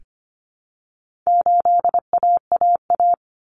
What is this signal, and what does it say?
Morse code, heard as a keyed tone, about 25 wpm, 700 Hz. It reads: 8AAA
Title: Common Cut Numbers - 4-Digit 25wpm